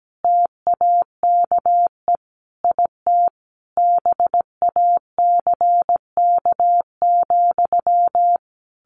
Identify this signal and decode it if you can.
Morse code, heard as a keyed tone, about 17 wpm, 700 Hz. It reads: TAKE IT BACK,
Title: A Sound Of Thunder 17wpm